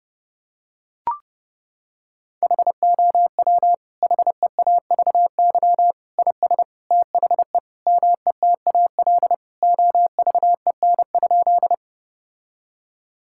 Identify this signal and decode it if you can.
Morse code, heard as a keyed tone, about 28 wpm, 700 Hz. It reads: HOW HEAVY IS THE METAL OVEN?
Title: Sentences from Top 1000 Words - Review Part 6 - 30wpm